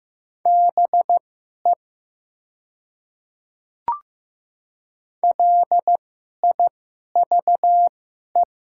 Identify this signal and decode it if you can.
Morse code, heard as a keyed tone, about 15 wpm, spaced as for 11 wpm, 700 Hz. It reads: BE LIVE
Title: 2x Character Spacing - Top 200 Words - 15wpm